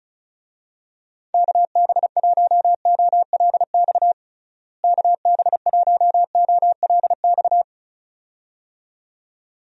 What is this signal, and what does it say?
Morse code, heard as a keyed tone, about 34 wpm, 700 Hz. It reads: KB1OLX KB1OLX
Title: USA Call Signs - 6 Characters - Repeated 2x - 35wpm